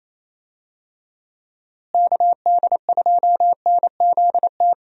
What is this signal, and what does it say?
Morse code, heard as a keyed tone, about 28 wpm, 700 Hz. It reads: KD2NZT
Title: USA Call Signs - 6 Characters - Repeated 2x - 28wpm